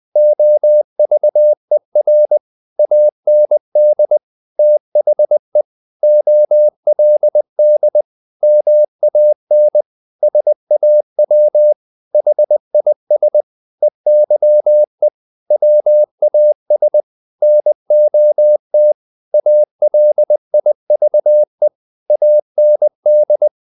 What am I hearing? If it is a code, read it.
Morse code, heard as a keyed tone, about 20 wpm, 600 Hz. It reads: OVER AND THE OLD MAN SAW HIS EYE WAS NOT ALIVE AND
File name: the old man and the sea part 2 20wpm